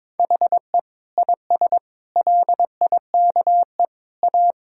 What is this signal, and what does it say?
Morse code, heard as a keyed tone, about 22 wpm, 700 Hz. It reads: HE IS LIKE A
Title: Sentences from Top 1000 Words - Review Part 1 - 22wpm